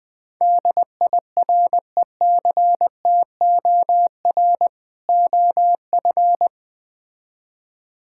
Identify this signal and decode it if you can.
Morse code, heard as a keyed tone, about 21 wpm, 700 Hz. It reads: DIRECTOR OF
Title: Build A Phrase - 3-Word Phrases - 20wpm